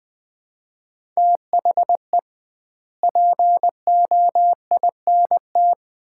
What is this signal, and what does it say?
Morse code, heard as a keyed tone, about 21 wpm, 700 Hz. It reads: THE POINT
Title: 2x Word-Spacing - Sets of 5 Words - 20wpm